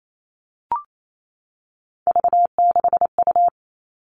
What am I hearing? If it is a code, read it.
Morse code, heard as a keyed tone, about 28 wpm, 700 Hz. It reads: V6U
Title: Three Challenging Letter-Number Combinations 28wpm